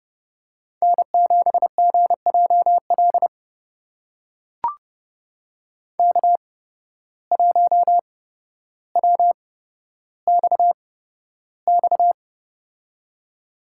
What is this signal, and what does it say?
Morse code, heard as a keyed tone, about 31 wpm, spaced as for 13 wpm, 700 Hz. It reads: N7GJL K1WXX
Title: USA Call Signs - 5 Characters - 8x Character Spacing - 30wpm